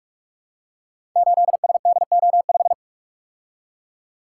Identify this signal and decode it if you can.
Morse code, heard as a keyed tone, about 44 wpm, 700 Hz. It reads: 8SDO5
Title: Code Groups - 5 Letter:Numbers - 45wpm